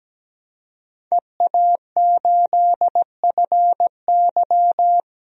Morse code, heard as a keyed tone, about 17 wpm, 700 Hz. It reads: EA8FY
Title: DXCC-100 Call Signs - 5 Characters - 17wpm